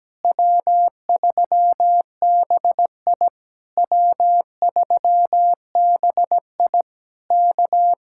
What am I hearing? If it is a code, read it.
Morse code, heard as a keyed tone, about 17 wpm, 700 Hz. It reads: W3BI W3BI K
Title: Speed-Racing Common QSO Phrases - 17wpm